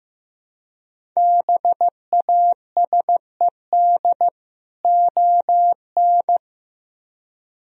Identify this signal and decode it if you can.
Morse code, heard as a keyed tone, about 17 wpm, 700 Hz. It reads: BASED ON
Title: Speed-Racing - Sets of 2 Words 15wpm